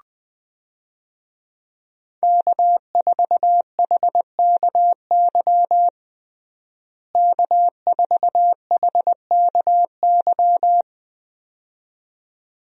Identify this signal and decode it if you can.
Morse code, heard as a keyed tone, about 20 wpm, 700 Hz. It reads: K4HKY K4HKY
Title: USA Call Signs - 5 Characters - Repeated 2x - 20wpm